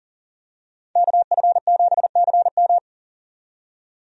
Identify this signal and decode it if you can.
Morse code, heard as a keyed tone, about 40 wpm, 700 Hz. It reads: KF7CM
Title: USA Call Signs - 5 Characters - 40wpm